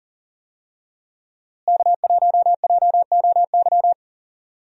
Morse code, heard as a keyed tone, about 40 wpm, 700 Hz. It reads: K1JOY
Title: USA Call Signs - 5 Characters - Repeated 2x - 40wpm